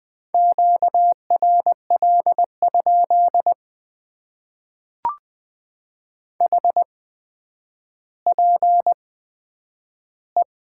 Morse code, heard as a keyed tone, about 20 wpm, spaced as for 10 wpm, 700 Hz. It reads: QRL? HPE
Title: Top 90 QSO Elements - 8x Character Spacing - 20wpm